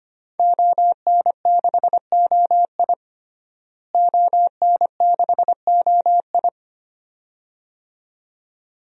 Morse code, heard as a keyed tone, about 25 wpm, 700 Hz. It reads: ON6OI ON6OI
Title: DXCC-100 Call Signs - 5 Characters - Repeated 2x - 25wpm